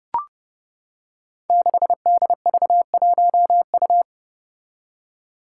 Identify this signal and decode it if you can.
Morse code, heard as a keyed tone, about 30 wpm, 700 Hz. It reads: 6DV1U
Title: Five Challenging Letter-Number Combinations 30wpm